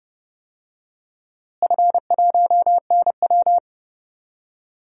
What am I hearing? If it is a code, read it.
Morse code, heard as a keyed tone, about 31 wpm, 700 Hz. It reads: F1NW
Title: DXCC-100 Call Signs - 4 Characters - Repeated 2x - 30wpm